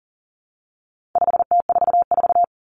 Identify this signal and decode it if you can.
Morse code, heard as a keyed tone, about 40 wpm, 700 Hz. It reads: 5T44
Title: Common Cut Numbers - 4-Digit 40wpm